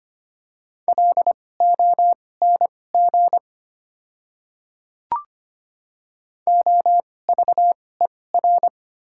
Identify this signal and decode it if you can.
Morse code, heard as a keyed tone, about 25 wpm, spaced as for 18 wpm, 700 Hz. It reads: LONG OVER
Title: 2x Character Spacing - Top 100 Words - 25wpm